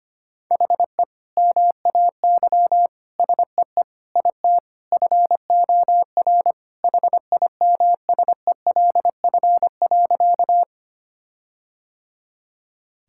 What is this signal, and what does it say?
Morse code, heard as a keyed tone, about 25 wpm, 700 Hz. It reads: HE MAY SEE IT FOR HIMSELF.
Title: Sentences from Top 1600 Words 25wpm